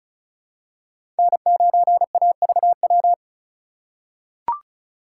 Morse code, heard as a keyed tone, about 36 wpm, 690 Hz. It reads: N9AVW
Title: USA Call Signs - 5 Characters - 3x Character Spacing - 35wpm